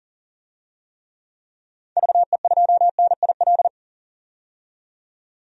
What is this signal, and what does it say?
Morse code, heard as a keyed tone, about 39 wpm, 700 Hz. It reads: VE2NIL
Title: DXCC-100 Call Signs - 6 Characters - 40wpm